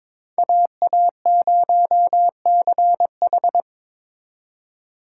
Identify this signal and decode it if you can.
Morse code, heard as a keyed tone, about 26 wpm, 700 Hz. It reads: AA0CH
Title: Speed Racing - USA 2x2 Call Signs 22wpm